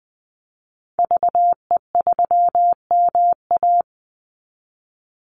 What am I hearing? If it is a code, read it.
Morse code, heard as a keyed tone, about 20 wpm, 700 Hz. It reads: VE3MA
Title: Call Signs Repeated 3x - 20wpm